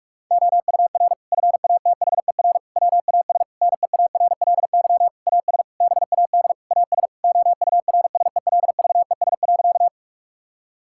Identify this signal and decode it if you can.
Morse code, heard as a keyed tone, about 46 wpm, 700 Hz. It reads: OUR FATHER WAS NEARLY AS BAD AS OURSELVES.
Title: Sentences from Top 1600 Words 45wpm